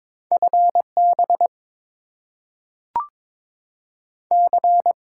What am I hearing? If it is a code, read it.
Morse code, heard as a keyed tone, about 22 wpm, 700 Hz. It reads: FB C
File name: Top 20 QSO Elements - 8x Character Spacing - 22wpm